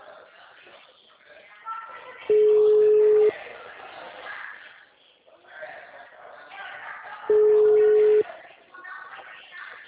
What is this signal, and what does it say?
Ringback tone (Europe, 425 Hz), with speech.
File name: out- -2015- - - .70841